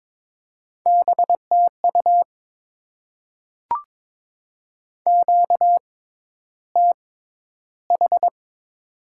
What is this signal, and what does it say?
Morse code, heard as a keyed tone, about 22 wpm, spaced as for 11 wpm, 700 Hz. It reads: BTU QTH
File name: Top 20 QSO Elements - 6x Character Spacing - 22wpm